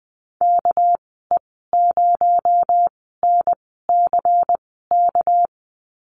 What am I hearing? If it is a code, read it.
Morse code, heard as a keyed tone, about 19 wpm, spaced as for 14 wpm, 700 Hz. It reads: KE0NCK
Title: 219 Lesson 35g - Number 0 - Call Signs 20wpm